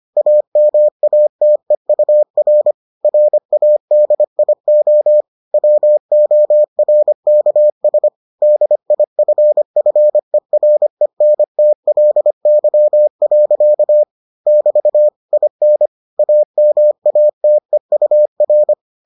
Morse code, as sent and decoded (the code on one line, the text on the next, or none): .- -- .- - . ..- .-. / .-. .- -.. .. --- / .-- --- .-. -.- ... / -.. .. ..-. ..-. .-. -. - .-.. -.-- .-.-.- / -...- / .. -. / .- -- .- - . ..- .-.
AMATEUR RADIO WORKS DIFFERENTLY. <BT> IN AMATEUR